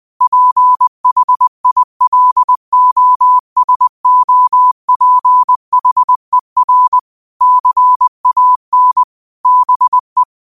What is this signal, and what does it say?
Morse code, heard as a keyed tone, about 20 wpm, 1000 Hz. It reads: PHILOSOPHER CAN BE
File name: qotdfiles / 20wpm